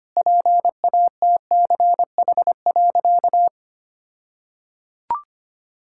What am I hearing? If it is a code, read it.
Morse code, heard as a keyed tone, about 25 wpm, 700 Hz. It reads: PATCH.
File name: practice / 25wpm / All-Age Appropriate Jokes - 25wpm